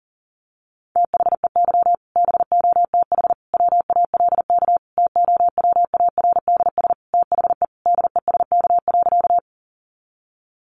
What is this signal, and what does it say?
Morse code, heard as a keyed tone, about 40 wpm, 700 Hz. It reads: THEY BOTH WALK TOWARDS THE DESK.
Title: Additional Easy Phrases 40wpm